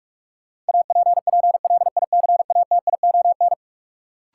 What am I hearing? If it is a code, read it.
Morse code, heard as a keyed tone, about 41 wpm, 700 Hz. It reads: APPLICATION
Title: Sentences from Top 1200 Words - New Word Drill Alt - 45wpm